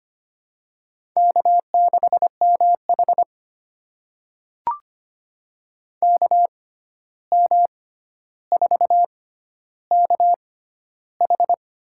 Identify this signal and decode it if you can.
Morse code, heard as a keyed tone, about 25 wpm, spaced as for 12 wpm, 700 Hz. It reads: K6MH KM4KH
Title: USA Call Signs - 6x Character Spacing - 25wpm